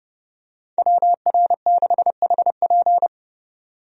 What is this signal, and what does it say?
Morse code, heard as a keyed tone, about 32 wpm, 700 Hz. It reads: WR6HP
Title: USA Call Signs - 5 Characters - 30wpm